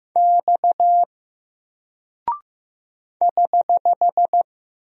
Morse code, heard as a keyed tone, about 15 wpm, 700 Hz. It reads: X <HH>
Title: Single Character (Letter:Number:Punctuation:Prosign) Alt 15wpm